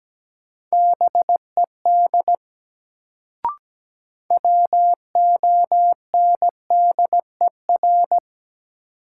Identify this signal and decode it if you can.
Morse code, heard as a keyed tone, about 17 wpm, 700 Hz. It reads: BED WONDER
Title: Sentences from Top 500 Words - New Word Drill - 17wpm